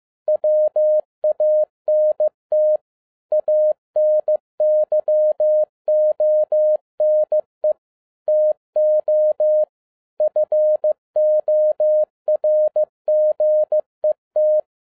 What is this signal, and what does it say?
Morse code, heard as a keyed tone, about 15 wpm, 610 Hz. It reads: WANT ANYONE TO FORGET
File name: FIELD DAY CONTEST QRV FROM SAKAE VILLAGE 15wpm